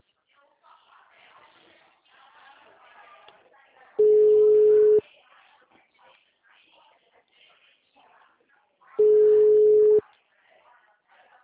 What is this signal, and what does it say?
Ringback tone (Europe, 425 Hz), with speech.
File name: out- -2012- - - .51486